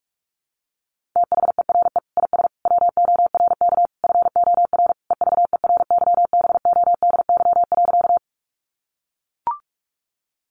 Morse code, heard as a keyed tone, about 44 wpm, 700 Hz. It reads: THERE IS WORK FOR EVERYBODY.
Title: Sentences from Top 1400 Words 45wpm